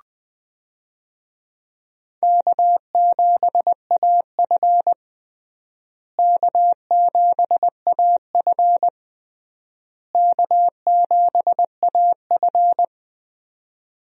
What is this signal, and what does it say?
Morse code, heard as a keyed tone, about 21 wpm, 700 Hz. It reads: K7AF K7AF K7AF
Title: USA Call Signs - 4 Characters - Repeated 3x - 20wpm